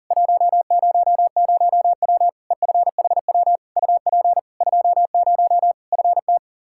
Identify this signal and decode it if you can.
Morse code, heard as a keyed tone, about 40 wpm, 700 Hz. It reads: 100W EFHW UP 20 FT
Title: K1USN SST QSO Practice 40wpm